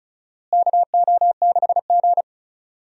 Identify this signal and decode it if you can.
Morse code, heard as a keyed tone, about 36 wpm, 700 Hz. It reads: KO6G
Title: USA Call Signs - 4 Characters - Repeated 3x - 35wpm